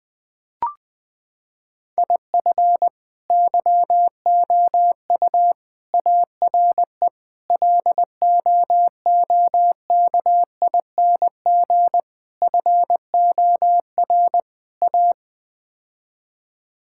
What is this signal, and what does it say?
Morse code, heard as a keyed tone, about 20 wpm, 700 Hz. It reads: IF YOU ARE LOOKING FOR A
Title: Sets of 6 Words - 20wpm